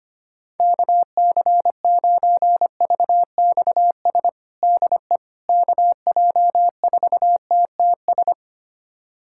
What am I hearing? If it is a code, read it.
Morse code, heard as a keyed tone, about 25 wpm, 700 Hz. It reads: KC9VXS DE KJ4TTS
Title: USA Callsign DE Callsign - 6 Characters - 25wpm